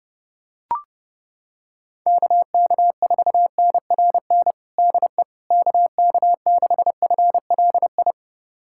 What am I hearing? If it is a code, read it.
Morse code, heard as a keyed tone, about 31 wpm, 700 Hz. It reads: KK4NRN DE KK6FLI
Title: USA Callsign DE Callsign - 6 Characters - 30wpm